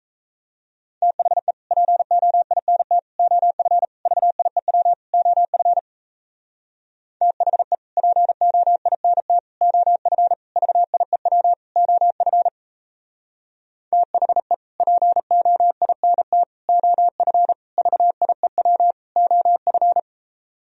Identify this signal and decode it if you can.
Morse code, heard as a keyed tone, about 36 wpm, 700 Hz. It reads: THE POINT OF VIEW OF THE POINT OF VIEW OF THE POINT OF VIEW OF